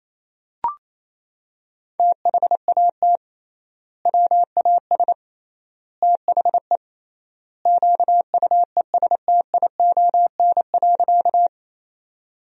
Morse code, heard as a keyed tone, about 28 wpm, 700 Hz. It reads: THAT WAS THE QUESTION.
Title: 3x Word-Spacing - Sentences from Top 400 Words - 28wpm